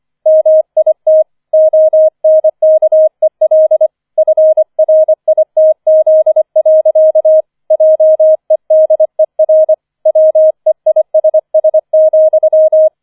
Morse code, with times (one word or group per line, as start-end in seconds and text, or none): 0.3-1.2 MIT
1.5-3.9 ONKEL
4.2-7.4 FRITZ.
7.7-9.7 JEDER
10.1-12.9 WEISS,